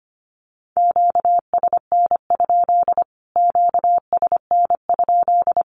Morse code, heard as a keyed tone, about 25 wpm, 700 Hz. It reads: QSN? QSN?
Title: Q-Codes Repeated 3x 25wpm